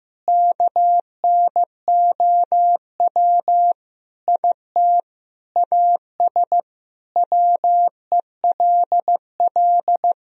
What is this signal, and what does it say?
Morse code, heard as a keyed tone, about 15 wpm, 700 Hz. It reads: KNOW IT AS WELL
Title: Sentences from Top 1000 Words - Review Part 3 - 15wpm